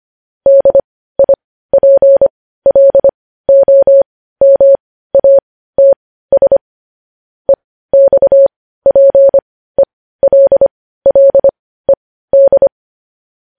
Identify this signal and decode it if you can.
Morse code, heard as a keyed tone, about 25 wpm, spaced as for 15 wpm, 550 Hz. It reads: DIPLOMATS EXPELLED
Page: News headlines in Morse code at 15 wpm